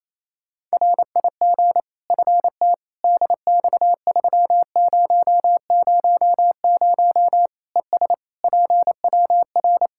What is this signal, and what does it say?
Morse code, heard as a keyed tone, about 28 wpm, 700 Hz. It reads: RIG FT DX3000 ES PWR
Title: Practice QSO - Exchange 2 - Info - 28wpm